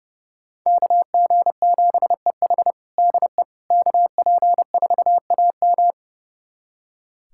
Morse code, heard as a keyed tone, about 30 wpm, 700 Hz. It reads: KG7EH DE KP4AM
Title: USA Callsign DE Callsign - 5 Characters - 30wpm